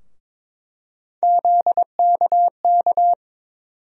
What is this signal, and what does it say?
Morse code, heard as a keyed tone, about 21 wpm, 700 Hz. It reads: ZKK
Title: USA Call Sign Postfixes - 3 Letters 22wpm